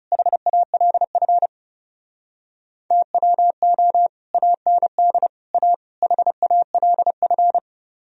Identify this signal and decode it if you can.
Morse code, heard as a keyed tone, about 38 wpm, 700 Hz. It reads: HALF TWO AND A HALF